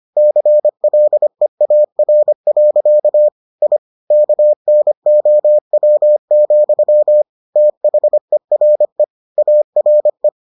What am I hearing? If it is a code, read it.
Morse code, heard as a keyed tone, about 25 wpm, 600 Hz. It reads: CLEAR. I KNOW, THERE ARE